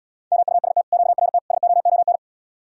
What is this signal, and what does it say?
Morse code, heard as a keyed tone, about 77 wpm, 700 Hz. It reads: MUST QRT SOON